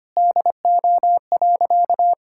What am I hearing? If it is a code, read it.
Morse code, heard as a keyed tone, about 25 wpm, 700 Hz. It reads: DO.
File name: practice / 25wpm / Sentences with 9-Word Phrases - 25wpm